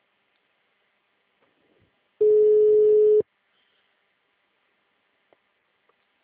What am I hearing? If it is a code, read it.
Ringback tone (Europe, 425 Hz).
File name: out- -2006- - - .448